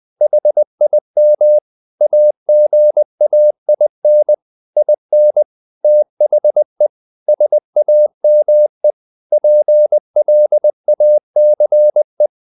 Morse code, heard as a keyed tone, about 20 wpm, 600 Hz. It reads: HIM AGAIN IN THE SAME PLACE
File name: the old man and the sea part 2 20wpm